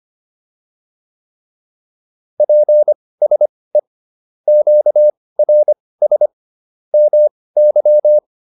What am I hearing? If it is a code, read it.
Morse code, heard as a keyed tone, about 25 wpm, spaced as for 18 wpm, 600 Hz. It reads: PSE QRS MY